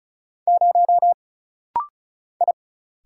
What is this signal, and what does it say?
Morse code, heard as a keyed tone, about 36 wpm, 700 Hz. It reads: 0 I
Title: Single Letter-Number - Rapid-Fire 35wpm